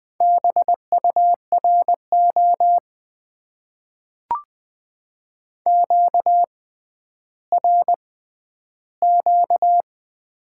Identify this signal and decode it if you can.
Morse code, heard as a keyed tone, about 20 wpm, spaced as for 11 wpm, 700 Hz. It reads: BURO QRQ
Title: Top 90 QSO Elements - 6x Character Spacing - 20wpm